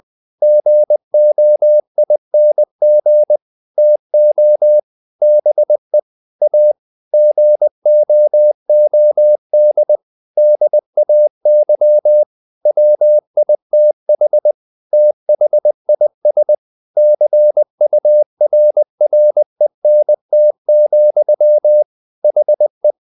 Morse code, heard as a keyed tone, about 20 wpm, 600 Hz. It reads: GOING TO BE A GOOD DAY WITH THIS CURRENT, HE
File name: the old man and the sea / the old man and the sea part 1 20wpm